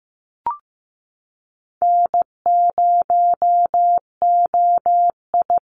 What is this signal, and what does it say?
Morse code, heard as a keyed tone, about 15 wpm, 700 Hz. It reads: N0OI
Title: NA SOTA Chaser Callsigns 15wpm